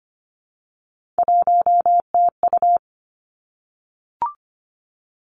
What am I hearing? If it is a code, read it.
Morse code, heard as a keyed tone, about 25 wpm, 700 Hz. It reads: 1TU
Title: Three Challenging Letter-Number Combinations 25wpm